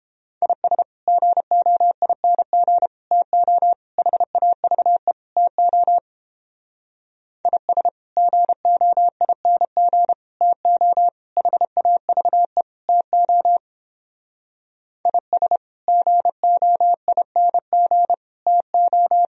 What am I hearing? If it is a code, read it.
Morse code, heard as a keyed tone, about 29 wpm, 700 Hz. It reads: IS GOING TO HAVE TO IS GOING TO HAVE TO IS GOING TO